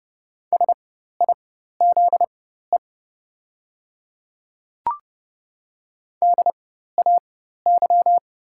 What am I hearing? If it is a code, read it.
Morse code, heard as a keyed tone, about 31 wpm, spaced as for 14 wpm, 700 Hz. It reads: SIZE DAY
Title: 4x Character Spacing - Top 500 Words - 30wpm